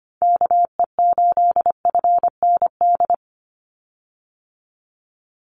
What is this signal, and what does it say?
Morse code, heard as a keyed tone, about 25 wpm, 700 Hz. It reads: KE8FND
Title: USA Call Signs 25wpm